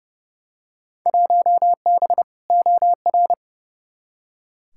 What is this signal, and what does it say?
Morse code, heard as a keyed tone, about 30 wpm, 700 Hz. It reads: 1B OR
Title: Field Day Exchanges - 30wpm